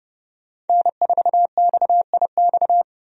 Morse code, heard as a keyed tone, about 32 wpm, 700 Hz. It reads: N4XIX
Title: USA Call Signs - 5 Characters - 30wpm